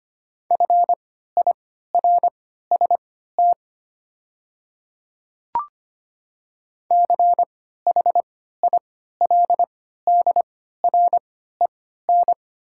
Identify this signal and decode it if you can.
Morse code, heard as a keyed tone, about 25 wpm, spaced as for 14 wpm, 700 Hz. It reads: FIRST CHILDREN
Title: 3x Character Spacing - Top 300 Words - 25wpm